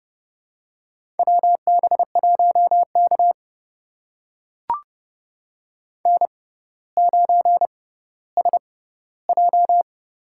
Morse code, heard as a keyed tone, about 30 wpm, spaced as for 15 wpm, 700 Hz. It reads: WB1K N9SJ
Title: USA Call Signs - 4 Characters - 6x Character Spacing - 30wpm